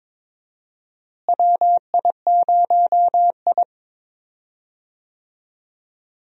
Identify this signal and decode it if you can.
Morse code, heard as a keyed tone, about 21 wpm, 700 Hz. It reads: WI0I
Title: USA Call Signs - 4 Characters - Repeated 2x - 22wpm